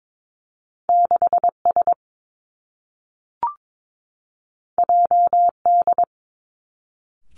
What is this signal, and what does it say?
Morse code, heard as a keyed tone, about 22 wpm, 700 Hz. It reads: 6 S J D
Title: Two Challenging Letter-Number Combinations 22wpm